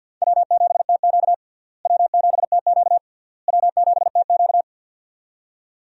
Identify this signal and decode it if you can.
Morse code, heard as a keyed tone, about 49 wpm, 700 Hz. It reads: W7TQ W7TQ W7TQ
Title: USA Call Signs - 4 Characters - Repeated 3x - 50wpm